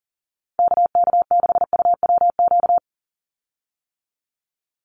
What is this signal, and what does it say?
Morse code, heard as a keyed tone, about 41 wpm, 700 Hz. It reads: KK6UWQ
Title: USA Call Signs 40wpm